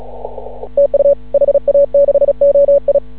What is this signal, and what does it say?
Morse code, heard as a keyed tone, about 36 wpm, 560 Hz. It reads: TU HA6OI
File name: CW / ha6oi 2